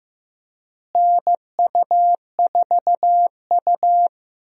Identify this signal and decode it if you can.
Morse code, heard as a keyed tone, about 15 wpm, 700 Hz. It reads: NU4U
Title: USA Call Signs - 4 Characters - Repeated 3x - 15wpm